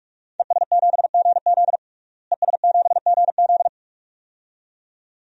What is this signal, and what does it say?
Morse code, heard as a keyed tone, about 43 wpm, 700 Hz. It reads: ES7GZ ES7GZ
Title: DXCC-100 Call Signs - 5 Characters - Repeated 2x - 45wpm